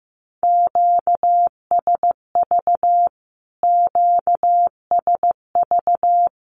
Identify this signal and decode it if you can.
Morse code, heard as a keyed tone, about 15 wpm, 700 Hz. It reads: QSV QSV
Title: Q-Codes Repeated 3x 15wpm